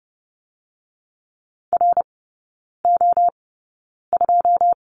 Morse code, heard as a keyed tone, about 30 wpm, spaced as for 9 wpm, 690 Hz. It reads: RO2
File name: Lesson 17e - Letter u - Three Characters 30wpm